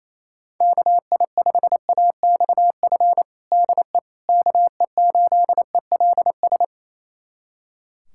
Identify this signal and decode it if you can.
Morse code, heard as a keyed tone, about 28 wpm, 700 Hz. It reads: KI5AXF DE KE8ELS
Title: USA Callsign DE Callsign - 6 Characters - 28wpm